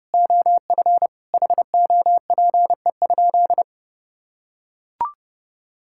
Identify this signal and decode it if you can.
Morse code, heard as a keyed tone, about 28 wpm, 700 Hz. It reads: OF HOPE?
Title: Sentences from Top 1000 Words - Review Part 5 - 30wpm